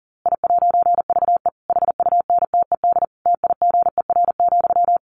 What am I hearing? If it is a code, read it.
Morse code, heard as a keyed tone, about 40 wpm, 700 Hz. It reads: I'VE HUNTED TIGER,
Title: A Sound Of Thunder 40wpm